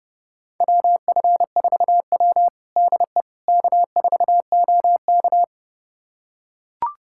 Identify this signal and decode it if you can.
Morse code, heard as a keyed tone, about 32 wpm, 700 Hz. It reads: WF4W DE K4OK
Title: USA Callsign DE Callsign - 4 Characters - 30wpm